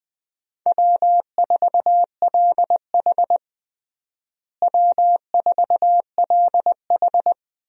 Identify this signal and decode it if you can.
Morse code, heard as a keyed tone, about 21 wpm, 700 Hz. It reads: W4LH W4LH
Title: USA Call Signs - 4 Characters - Repeated 2x - 20wpm